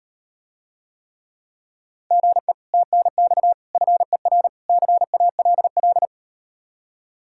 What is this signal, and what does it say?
Morse code, heard as a keyed tone, about 38 wpm, 700 Hz. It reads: GE TNX FER CALL